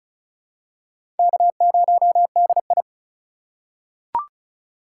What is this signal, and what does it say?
Morse code, heard as a keyed tone, about 36 wpm, 700 Hz. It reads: K0DI
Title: USA Call Signs - 4 Characters - 2x Character Spacing - 35wpm